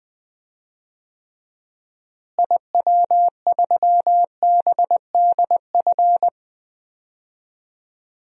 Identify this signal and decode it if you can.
Morse code, heard as a keyed tone, about 20 wpm, 700 Hz. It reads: IW3BDF
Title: DXCC-100 Call Signs - 6 Characters - Repeated 3x - 20wpm